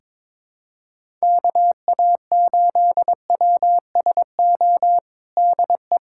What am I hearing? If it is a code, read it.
Morse code, heard as a keyed tone, about 22 wpm, 700 Hz. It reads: KA8WSO DE
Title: USA Callsign DE Callsign - 6 Characters - 22wpm